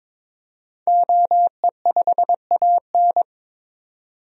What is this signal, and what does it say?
Morse code, heard as a keyed tone, about 22 wpm, 700 Hz. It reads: OE5AN
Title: DXCC-100 Call Signs - 5 Characters - 22wpm